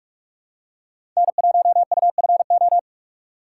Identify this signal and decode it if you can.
Morse code, heard as a keyed tone, about 41 wpm, 700 Hz. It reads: N1UFO
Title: USA Call Signs - 5 Characters - 8x Character Spacing - 45wpm